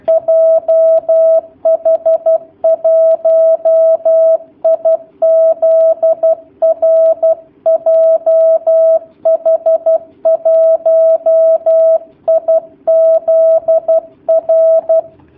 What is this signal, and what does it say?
Morse code, heard as a keyed tone, about 12 wpm, 650 Hz. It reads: JH1IZR JH1IZR
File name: jh8cbh / 2019cw / sp11- w